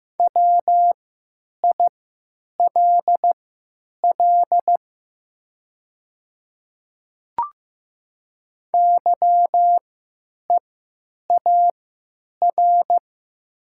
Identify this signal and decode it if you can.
Morse code, heard as a keyed tone, about 15 wpm, spaced as for 9 wpm, 700 Hz. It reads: WILL YEAR
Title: 3x Character Spacing - Top 300 Words - 15wpm